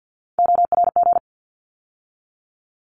Morse code, heard as a keyed tone, about 50 wpm, 700 Hz. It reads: ORG
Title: ARRL Sections 50wpm